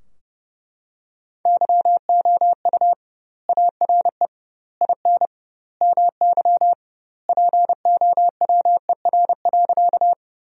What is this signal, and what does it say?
Morse code, heard as a keyed tone, about 30 wpm, 700 Hz. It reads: YOU ARE IN MY POWER.
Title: 2x Word-Spacing - Sentences from Top 500 Words - 30wpm